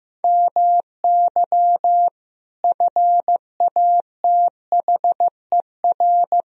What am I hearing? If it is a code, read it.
Morse code, heard as a keyed tone, about 15 wpm, 700 Hz. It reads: MY FATHER
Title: Sentences from Top 1000 Words - Review Part 1 - 15wpm